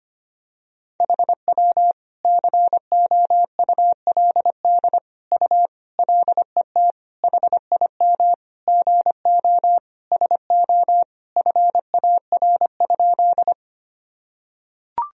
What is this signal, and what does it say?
Morse code, heard as a keyed tone, about 25 wpm, 700 Hz. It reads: HW COULD U LET HIM GO SO FAR?
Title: Sentences from Top 300 Words and Abbreviations 25wpm